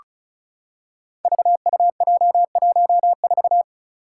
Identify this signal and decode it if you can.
Morse code, heard as a keyed tone, about 36 wpm, 700 Hz. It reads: VUJ14
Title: Five Challenging Letter-Number Combinations 35wpm